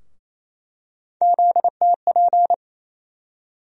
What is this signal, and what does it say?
Morse code, heard as a keyed tone, about 28 wpm, 700 Hz. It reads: ZTP
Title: USA Call Sign Postfixes - 3 Letters 28wpm